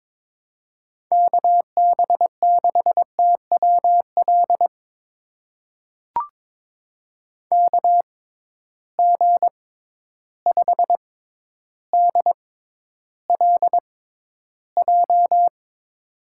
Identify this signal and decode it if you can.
Morse code, heard as a keyed tone, about 22 wpm, spaced as for 11 wpm, 700 Hz. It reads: KB6TWL KG5DLJ
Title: USA Call Signs - 6 Characters - 6x Character Spacing - 22wpm